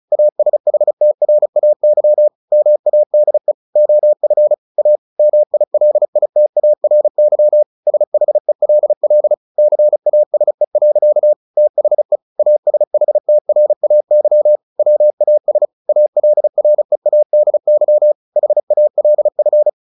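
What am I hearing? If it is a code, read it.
Morse code, heard as a keyed tone, about 34 wpm, 600 Hz. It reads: ASHTRAY MADE OF A MILITARY SHELL CASE. THE ASHTRAY WAS ALREADY HALF
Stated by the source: Morse practice at 35 wpm from CW med Gnister, sending Ken Follett's novel Never in English.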